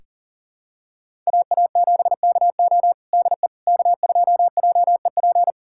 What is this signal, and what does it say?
Morse code, heard as a keyed tone, about 37 wpm, 700 Hz. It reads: AA7KO DE K2JEP
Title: USA Callsign DE Callsign - 5 Characters - 40wpm